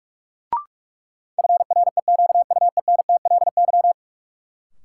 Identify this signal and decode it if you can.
Morse code, heard as a keyed tone, about 46 wpm, 700 Hz. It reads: FREQUENTLY
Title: Sentences from Top 1300 Words - New Word Drill Alt - 45wpm